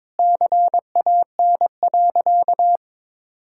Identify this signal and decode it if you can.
Morse code, heard as a keyed tone, about 22 wpm, 700 Hz. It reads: CAN.
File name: practice / 22wpm / Sentences with 2-Word Phrases - 22wpm